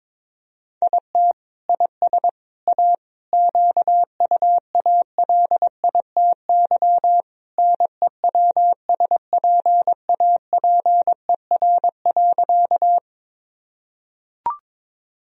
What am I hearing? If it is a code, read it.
Morse code, heard as a keyed tone, about 22 wpm, 700 Hz. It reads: IT IS A QUALITY NEWSPAPER.
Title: Sentences from Top 1000 Words - Review Part 4 - 22wpm